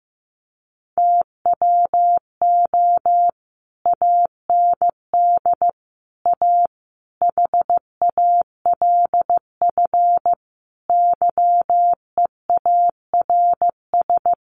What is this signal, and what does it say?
Morse code, heard as a keyed tone, about 15 wpm, 700 Hz. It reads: TWO AND A HALF YEARS
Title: Sets of 5 Words 15wpm